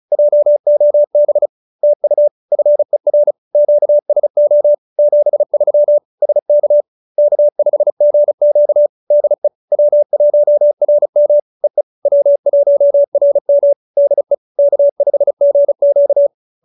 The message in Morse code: JOB TU FER QSO 73 SK K5GQ DE W1RM EE W1RM DE K5GQ

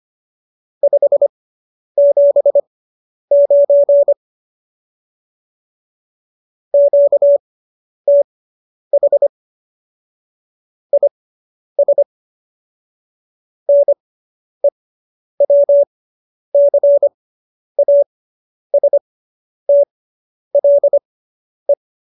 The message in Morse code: 579 QTH IS NEWCASTLE

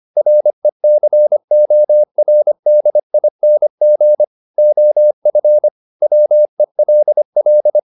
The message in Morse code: RECORDING OF WELL

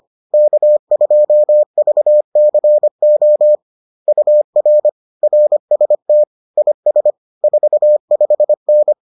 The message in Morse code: K2VCO UR RST IS 45N

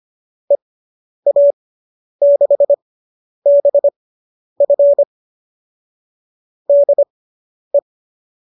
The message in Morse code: EA6BF DE

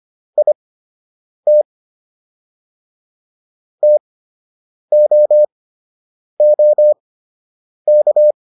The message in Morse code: IT TOOK

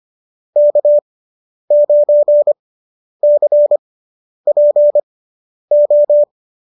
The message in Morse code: K9CPO

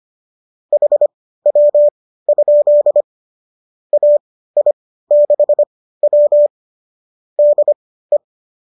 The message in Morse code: HW? AI6W DE

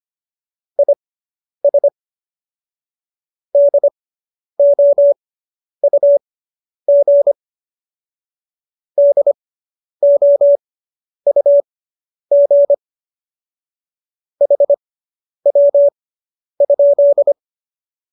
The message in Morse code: IS DOUG DOUG HW?

